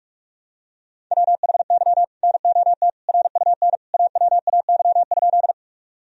Morse code, heard as a keyed tone, about 41 wpm, 700 Hz. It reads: WHY NOT RUN AWAY?
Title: Sentences from Top 1000 Words - Review Part 1 - 45wpm